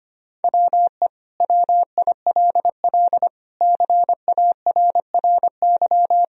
Morse code, heard as a keyed tone, about 25 wpm, 700 Hz. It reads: WE WILL CARRY
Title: Sentences from Top 1000 Words - Review Part 1 - 25wpm